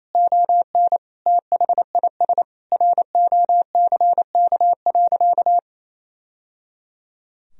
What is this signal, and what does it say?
Morse code, heard as a keyed tone, about 28 wpm, 700 Hz. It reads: ON THIS ROCK.
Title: Sentences from Top 1000 Words - Review Part 2 - 28wpm